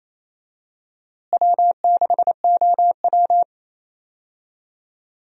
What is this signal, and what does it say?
Morse code, heard as a keyed tone, about 28 wpm, 700 Hz. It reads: W6OW
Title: USA Call Signs - 4 Characters - Repeated 3x - 28wpm